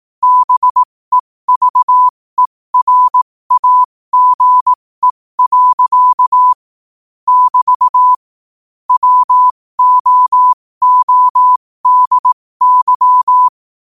Morse code, heard as a keyed tone, about 18 wpm, spaced as for 15 wpm, 1000 Hz. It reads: BEVERAGE. <BT> WOODY
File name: qotdfiles / 15wpm